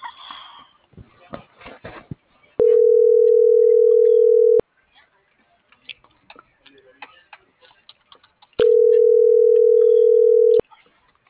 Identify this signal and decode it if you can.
Ringback tone (North America), 440 + 480 Hz.